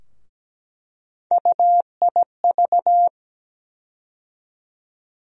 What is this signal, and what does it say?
Morse code, heard as a keyed tone, about 17 wpm, 700 Hz. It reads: UIV
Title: USA Call Sign Postfixes - 3 Letters 17wpm